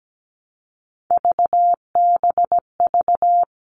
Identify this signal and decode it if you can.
Morse code, heard as a keyed tone, about 17 wpm, 700 Hz. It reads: VBV
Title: All Cut Numbers - 3-Digit 17wpm